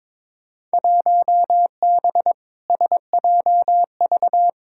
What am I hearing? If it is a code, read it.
Morse code, heard as a keyed tone, about 22 wpm, 700 Hz. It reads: 1B SJV
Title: Field Day Exchanges - 22wpm